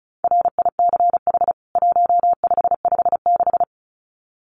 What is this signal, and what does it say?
Morse code, heard as a keyed tone, about 34 wpm, 700 Hz. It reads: RICH 1556
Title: CWT Exchange 35wpm